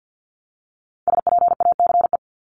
Morse code, heard as a keyed tone, about 50 wpm, 700 Hz. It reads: SPACE
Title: Top 501-750 Words 50wpm